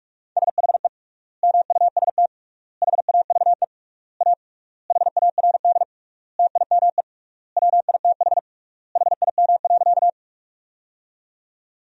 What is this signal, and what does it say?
Morse code, heard as a keyed tone, about 46 wpm, 700 Hz. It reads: SHE MUST HAVE A HARD TIME WITH HIM.
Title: 3x Word-Spacing - Sentences from Top 300 Words - 45wpm